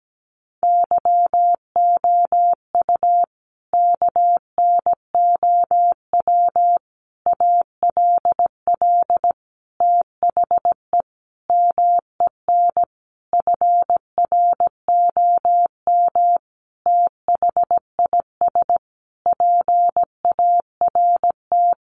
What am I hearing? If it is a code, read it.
Morse code, heard as a keyed tone, about 17 wpm, 700 Hz. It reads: YOU KNOW ALL THE MEN FROM THIS PART
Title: Sentences from Top 300 Words 17wpm